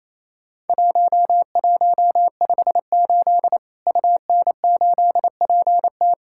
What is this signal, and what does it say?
Morse code, heard as a keyed tone, about 27 wpm, 700 Hz. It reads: 1158 UN8PT
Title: Worked All Europe DX Contest - QTC Exchange - 28wpm